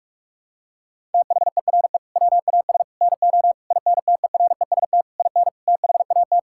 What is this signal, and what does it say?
Morse code, heard as a keyed tone, about 41 wpm, 700 Hz. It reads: THERE WAS NO INTEREST IN THAT
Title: Sentences from Top 1000 Words - Review Part 5 - 45wpm